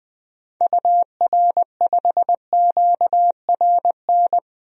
Morse code, heard as a keyed tone, about 20 wpm, 700 Hz. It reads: UR5QRN
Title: DXCC-100 Call Signs - 6 Characters - 20wpm